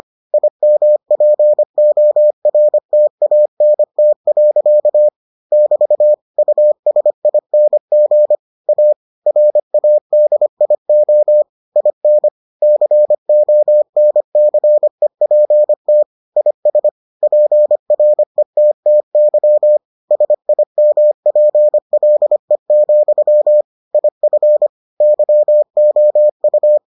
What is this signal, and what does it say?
Morse code, heard as a keyed tone, about 25 wpm, 600 Hz. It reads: IMPORTANT. <BT> USING A RADIO IN CONCEPT IS PRETTY SIMPLE, IF YOU